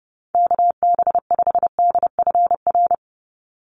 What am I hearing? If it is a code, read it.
Morse code, heard as a keyed tone, about 30 wpm, 700 Hz. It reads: KB5DFR
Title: USA Call Signs 30wpm